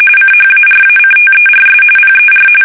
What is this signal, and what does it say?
RTTY, about 45.45 baud. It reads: CHARACTERS PER
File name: fsk850 / teletype-corp